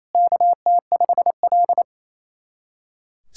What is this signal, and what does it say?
Morse code, heard as a keyed tone, about 28 wpm, 700 Hz. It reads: KT5L
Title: SOTA QSO Practice 28wpm